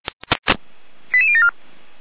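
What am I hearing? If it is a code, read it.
Five-tone selective call (ZVEI), heard as quick runs of five tones, tones of about 70 ms. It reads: 88074
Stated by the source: SDR-received ZVEI five-tone call 88074.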